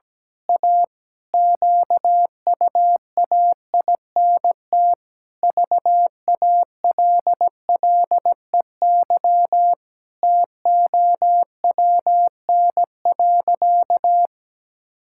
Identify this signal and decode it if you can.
Morse code, heard as a keyed tone, about 17 wpm, 700 Hz. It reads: A QUAINT VALLEY TOWN.
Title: Pangram Challenge v2 - 17wpm